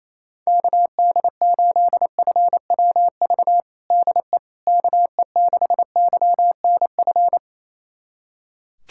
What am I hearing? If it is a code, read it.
Morse code, heard as a keyed tone, about 28 wpm, 700 Hz. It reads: KD8FWV DE KE6YNF
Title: USA Callsign DE Callsign 28wpm